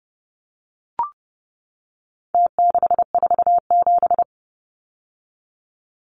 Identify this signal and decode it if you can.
Morse code, heard as a keyed tone, about 30 wpm, 700 Hz. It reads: T647
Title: Numbers - Contest Style - 4-Digit 30wpm